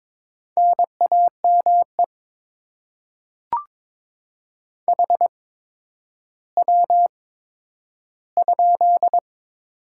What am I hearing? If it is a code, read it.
Morse code, heard as a keyed tone, about 22 wpm, spaced as for 11 wpm, 700 Hz. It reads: NAME HW?
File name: Top 20 QSO Elements - 8x Character Spacing - 22wpm